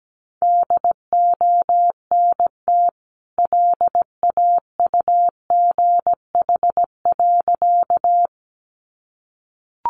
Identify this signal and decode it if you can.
Morse code, heard as a keyed tone, about 17 wpm, 700 Hz. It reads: DONT LAUGH.
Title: Sentences from Top 500 Words 17wpm